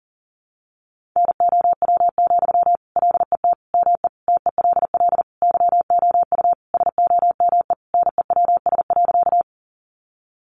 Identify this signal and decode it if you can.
Morse code, heard as a keyed tone, about 40 wpm, 700 Hz. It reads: NOW, LET ME TELL YOU SOME NEWS.
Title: Sentences from Top 800 Words 40wpm